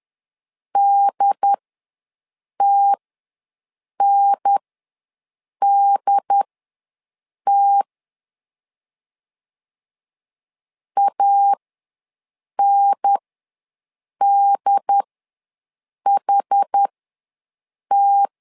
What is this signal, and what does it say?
Morse code, heard as a keyed tone, about 10 wpm, spaced as for 6 wpm, 800 Hz. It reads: DTNDT ANDHT
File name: LEZIONI / 2 UHDN / 2 UHDN ws 10wpm 2044